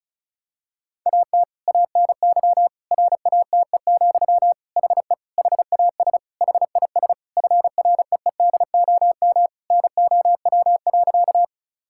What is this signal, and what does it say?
Morse code, heard as a keyed tone, about 34 wpm, 700 Hz. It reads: AT ANY RATE, HE HAS HIS FREEDOM NOW.
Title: Sentences from Top 1000 Words - Review Part 5 - 35wpm